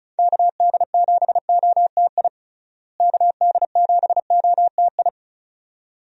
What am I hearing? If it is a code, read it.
Morse code, heard as a keyed tone, about 36 wpm, 700 Hz. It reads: KD7OTI KD7OTI
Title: USA Call Signs - 6 Characters - Repeated 2x - 35wpm